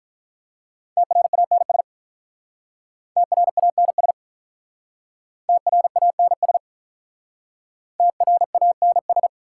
Morse code, heard as a keyed tone, about 45 wpm, 700 Hz. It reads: TRANS TRANS TRANS TRANS